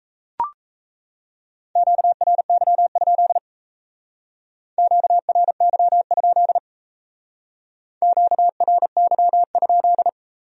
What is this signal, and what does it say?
Morse code, heard as a keyed tone, about 36 wpm, 700 Hz. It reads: QRY? QRY? QRY?